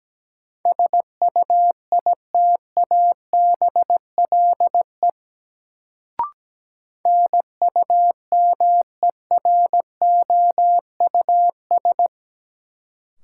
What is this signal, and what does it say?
Morse code, heard as a keyed tone, about 17 wpm, 700 Hz. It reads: SUITABLE NUMEROUS
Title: Sentences from Top 1400 Words - New Word Drill - 17wpm